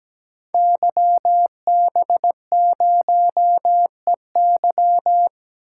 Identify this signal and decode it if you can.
Morse code, heard as a keyed tone, about 17 wpm, 700 Hz. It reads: YB0EY
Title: Code Groups - 5 Letter:Numbers - 17wpm